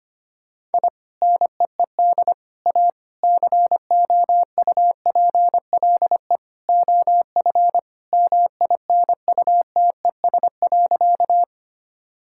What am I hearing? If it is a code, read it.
Morse code, heard as a keyed tone, about 25 wpm, 700 Hz. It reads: I NEED A COUPLE OF MINUTES.
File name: practice / 25wpm / Sentences with 3-Word Phrases - 25wpm